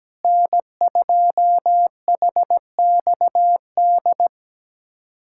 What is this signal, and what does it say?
Morse code, heard as a keyed tone, about 17 wpm, 700 Hz. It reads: N2HXD
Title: USA Callsign DE Callsign - 5 Characters - 17wpm